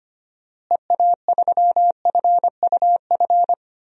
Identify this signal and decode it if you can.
Morse code, heard as a keyed tone, about 25 wpm, 700 Hz. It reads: EA3FUF
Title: DXCC-100 Call Signs - 6 Characters - Repeated 3x - 25wpm